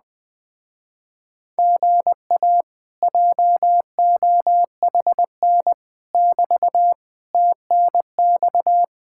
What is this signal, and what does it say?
Morse code, heard as a keyed tone, about 20 wpm, 700 Hz. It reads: GA JOHN <BT> TNX
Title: Speed-Racing Common QSO Phrases Encore - 17wpm